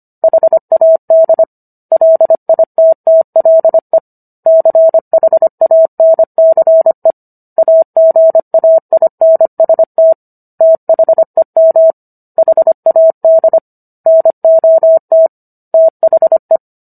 Morse code, heard as a keyed tone, about 25 wpm, 650 Hz. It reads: HAD LITTLE CHANCE AGAINST THEM HAD NOT THE